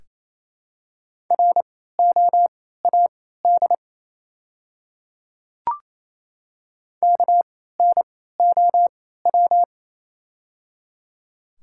Morse code, heard as a keyed tone, about 28 wpm, spaced as for 16 wpm, 700 Hz. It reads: ROAD KNOW
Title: 3x Character Spacing - Top 500 Words - 28wpm